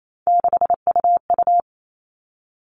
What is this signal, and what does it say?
Morse code, heard as a keyed tone, about 28 wpm, 700 Hz. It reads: 6UU
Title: All Cut Numbers - 3-Digit 28wpm